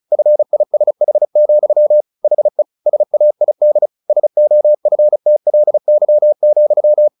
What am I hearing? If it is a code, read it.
Morse code, heard as a keyed tone, about 34 wpm, 600 Hz. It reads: FISH, HE SAID SOFTLY,
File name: the old man and the sea / the old man and the sea 1 35wpm